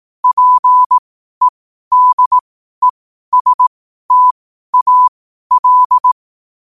Morse code, heard as a keyed tone, about 18 wpm, spaced as for 12 wpm, 1000 Hz. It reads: PEDESTAL
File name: qotdfiles / 12wpm